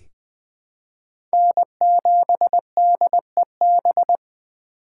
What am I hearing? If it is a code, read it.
Morse code, heard as a keyed tone, about 20 wpm, 700 Hz. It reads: N7DEB
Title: USA Call Signs - 5 Characters - 4x Character Spacing - 20wpm